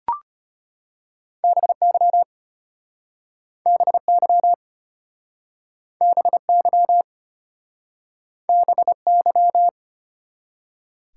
Morse code, heard as a keyed tone, about 34 wpm, 700 Hz. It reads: BY BY BY BY